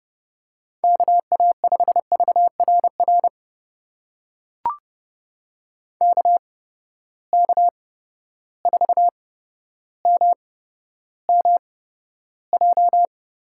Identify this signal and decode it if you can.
Morse code, heard as a keyed tone, about 30 wpm, spaced as for 13 wpm, 700 Hz. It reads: KA5VRR KK4MMJ
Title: USA Call Signs - 6 Characters - 8x Character Spacing - 30wpm